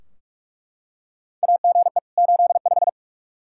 Morse code, heard as a keyed tone, about 42 wpm, 700 Hz. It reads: AGE 85
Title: Practice QSO - Exchange 3 - More Info - 45wpm